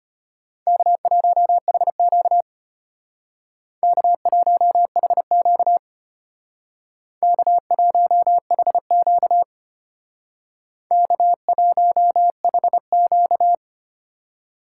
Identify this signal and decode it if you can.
Morse code, heard as a keyed tone, about 29 wpm, 700 Hz. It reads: K1HQ K1HQ K1HQ K1HQ
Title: Speed Racing - USA 1x2 Call Signs 25wpm